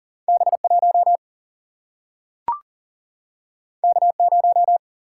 Morse code, heard as a keyed tone, about 39 wpm, 700 Hz. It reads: B 1 K 0
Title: Single-Letter Single-Number 40wpm